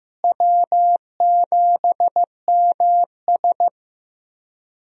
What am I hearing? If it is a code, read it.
Morse code, heard as a keyed tone, about 16 wpm, 700 Hz. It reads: W7MS
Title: USA Call Signs - 4 Characters - Repeated 3x - 15wpm